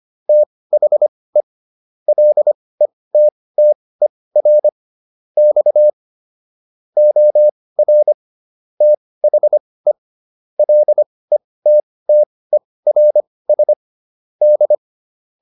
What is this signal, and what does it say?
Morse code, heard as a keyed tone, about 25 wpm, spaced as for 18 wpm, 600 Hz. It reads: THE LETTER X OR THE LETTERS D